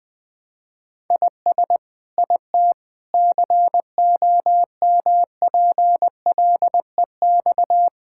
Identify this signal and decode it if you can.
Morse code, heard as a keyed tone, about 20 wpm, 700 Hz. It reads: IS IT COMPLEX
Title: Sentences from Top 1000 Words - Review Part 6 - 20wpm